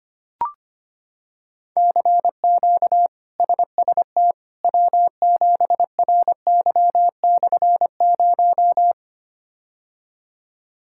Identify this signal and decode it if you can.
Morse code, heard as a keyed tone, about 25 wpm, 700 Hz. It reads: CQ SST W7RY/0
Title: K1USN SST QSO Practice - 25wpm